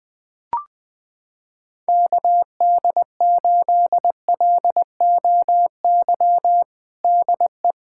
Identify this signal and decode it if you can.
Morse code, heard as a keyed tone, about 21 wpm, 700 Hz. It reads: KD8LOY DE
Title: USA Callsign DE Callsign - 6 Characters - 20wpm